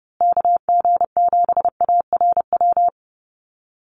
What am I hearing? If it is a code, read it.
Morse code, heard as a keyed tone, about 30 wpm, 700 Hz. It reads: KG7ARW
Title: USA Call Signs Repeated 3x 30wpm